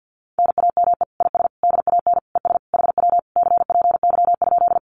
Morse code, heard as a keyed tone, about 50 wpm, 700 Hz. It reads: NAME IS DAN ES HW CPY?
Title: Common QSO Phrases Encore 50wpm